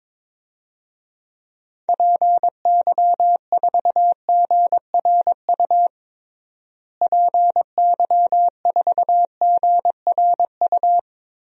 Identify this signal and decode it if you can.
Morse code, heard as a keyed tone, about 22 wpm, 700 Hz. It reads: PY4GRU PY4GRU
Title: DXCC-100 Call Signs - 6 Characters - Repeated 2x - 22wpm